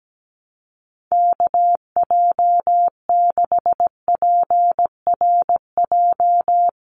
Morse code, heard as a keyed tone, about 17 wpm, 700 Hz. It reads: KJ6PRJ
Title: USA Call Signs 17wpm